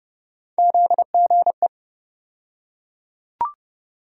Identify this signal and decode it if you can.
Morse code, heard as a keyed tone, about 30 wpm, 700 Hz. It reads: ZGE
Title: USA Call Sign Postfixes - 3 Letters 30wpm